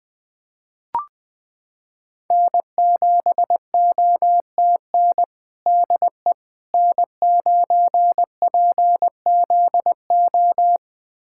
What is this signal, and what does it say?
Morse code, heard as a keyed tone, about 20 wpm, 700 Hz. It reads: N7OTN DE N9PZO
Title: USA Callsign DE Callsign - 5 Characters - 20wpm